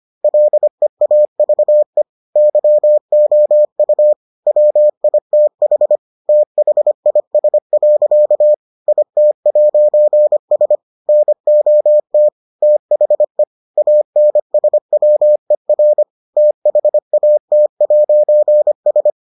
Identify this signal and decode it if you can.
Morse code, heard as a keyed tone, about 25 wpm, 600 Hz. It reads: LEAVE YOU WITH THIS. IT'S NOT THE ANSWER THAT'S